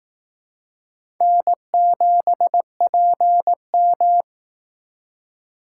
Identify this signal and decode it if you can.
Morse code, heard as a keyed tone, about 18 wpm, 700 Hz. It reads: N7PM
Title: Speed Racing - USA 1x2 Call Signs 15wpm